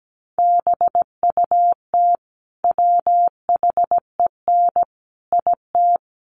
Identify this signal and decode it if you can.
Morse code, heard as a keyed tone, about 17 wpm, 700 Hz. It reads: BUT WHEN IT
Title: Sets of 5 Words 17wpm